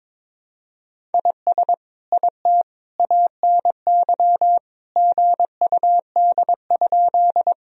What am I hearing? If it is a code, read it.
Morse code, heard as a keyed tone, about 22 wpm, 700 Hz. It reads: IS IT ANY GUD?
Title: Sentences from Top 200 Words and Abbreviations 22wpm